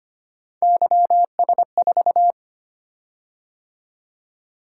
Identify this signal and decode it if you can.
Morse code, heard as a keyed tone, about 25 wpm, 700 Hz. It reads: YS4
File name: Three Letter-Number Combinations - 25wpm